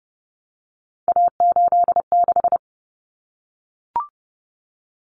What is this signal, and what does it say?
Morse code, heard as a keyed tone, about 30 wpm, 700 Hz. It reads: A86
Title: Common Cut Numbers - 3-Digit 30wpm